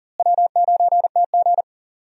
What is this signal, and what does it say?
Morse code, heard as a keyed tone, about 40 wpm, 700 Hz. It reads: W9TG
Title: USA Call Signs - 4 Characters - Repeated 2x - 40wpm